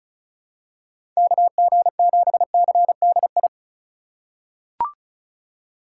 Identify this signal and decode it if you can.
Morse code, heard as a keyed tone, about 34 wpm, 700 Hz. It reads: KG7CDI
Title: USA Call Signs - 6 Characters - 2x Character Spacing - 35wpm